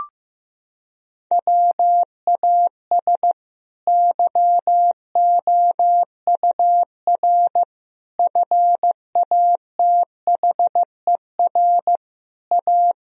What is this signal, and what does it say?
Morse code, heard as a keyed tone, about 16 wpm, 700 Hz. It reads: WAS YOUR FATHER A
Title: Sentences from Top 1000 Words - Review Part 4 - 15wpm